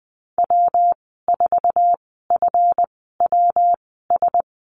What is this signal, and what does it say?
Morse code, heard as a keyed tone, about 20 wpm, spaced as for 14 wpm, 700 Hz. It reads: W4FWS
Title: 147 Lesson 24g - Letter y - Call Signs 20wpm